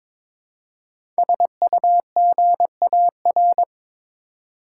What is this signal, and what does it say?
Morse code, heard as a keyed tone, about 22 wpm, 700 Hz. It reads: SUGAR
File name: 5 Letter Words - First 3 Letters Given - 22wpm